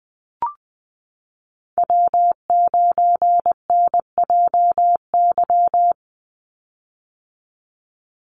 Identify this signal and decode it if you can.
Morse code, heard as a keyed tone, about 20 wpm, 700 Hz. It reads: W9NJY
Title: Call Signs - 20wpm